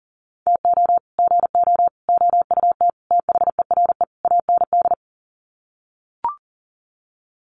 Morse code, heard as a keyed tone, about 40 wpm, 700 Hz. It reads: TO GO OUT THERE AND
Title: Sets of 5 Words Encore2 40wpm